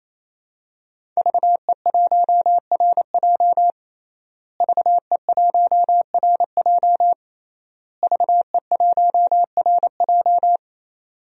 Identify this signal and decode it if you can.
Morse code, heard as a keyed tone, about 27 wpm, 700 Hz. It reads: VE1RJ VE1RJ VE1RJ
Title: DXCC-100 Call Signs - 5 Characters - Repeated 3x - 28wpm